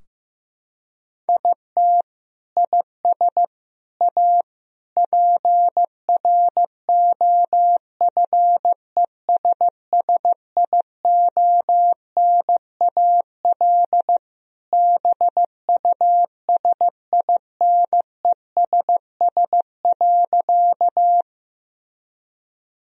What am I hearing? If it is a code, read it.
Morse code, heard as a keyed tone, about 16 wpm, 700 Hz. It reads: IT IS A PROFESSIONAL BUSINESS.
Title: Sentences from Top 1000 Words - Review Part 4 - 15wpm